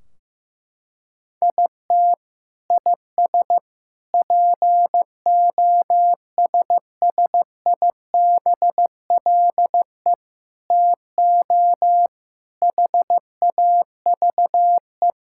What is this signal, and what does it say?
Morse code, heard as a keyed tone, about 16 wpm, 700 Hz. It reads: IT IS POSSIBLE TO HAVE
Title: Sentences from Top 1600 Words 15wpm